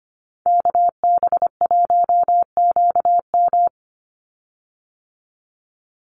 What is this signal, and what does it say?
Morse code, heard as a keyed tone, about 25 wpm, 690 Hz. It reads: KB1QM
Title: USA Call Signs 25wpm